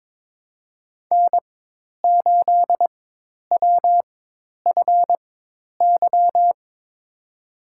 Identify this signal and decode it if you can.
Morse code, heard as a keyed tone, about 22 wpm, spaced as for 10 wpm, 700 Hz. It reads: N8WFY
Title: USA Call Signs - 5 Characters - 4x Character Spacing - 22wpm